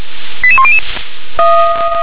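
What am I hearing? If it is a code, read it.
Five-tone selective call (ZVEI), heard as quick runs of five tones, tones of about 70 ms. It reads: 88100
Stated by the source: ZVEI five-tone selective call received by SDR.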